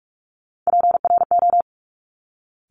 Morse code, heard as a keyed tone, about 46 wpm, 690 Hz. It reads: PRO
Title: Top Three Letter Combinations in Words 45wpm